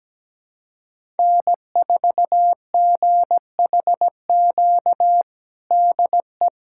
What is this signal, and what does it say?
Morse code, heard as a keyed tone, about 17 wpm, 700 Hz. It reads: N4GHQ DE
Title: USA Callsign DE Callsign - 5 Characters - 17wpm